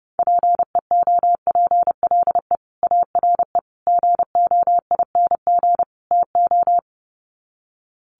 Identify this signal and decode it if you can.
Morse code, heard as a keyed tone, about 30 wpm, 700 Hz. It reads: PEOPLE ARE GOING TO
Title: Sets of 4 Words 30wpm